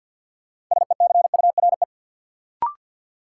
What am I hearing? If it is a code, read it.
Morse code, heard as a keyed tone, about 54 wpm, 700 Hz. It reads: SECURE
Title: Sentences from Top 1200 Words - New Word Drill Alt - 50wpm